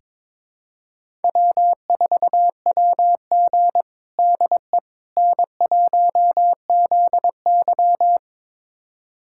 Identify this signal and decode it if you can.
Morse code, heard as a keyed tone, about 22 wpm, 700 Hz. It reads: W4WG DE N1ZY
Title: USA Callsign DE Callsign - 4 Characters - 22wpm